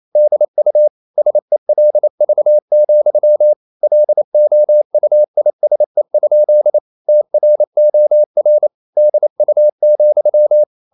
Morse code, heard as a keyed tone, about 27 wpm, 600 Hz. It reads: DU SELV, LOUISE? TROR DU,
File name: Ferie QRQ 28wpm